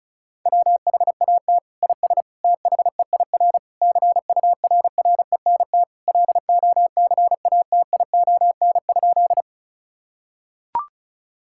Morse code, heard as a keyed tone, about 36 wpm, 700 Hz. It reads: WHAT IS THEIR CURRENT LOCATION?
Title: Sentences from Top 1000 Words - Review Part 6 - 35wpm